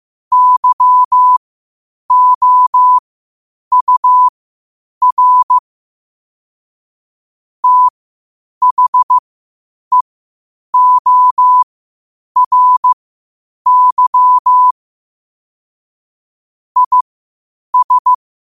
Morse code, heard as a keyed tone, about 15 wpm, spaced as for 8 wpm, 1000 Hz. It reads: YOUR THEORY IS
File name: qotdfiles / 8wpm